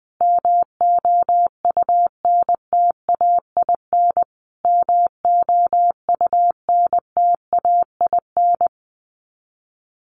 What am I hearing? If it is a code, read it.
Morse code, heard as a keyed tone, about 20 wpm, 700 Hz. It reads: MOUNTAIN MOUNTAIN
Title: Top 500 Words Repeated 20wpm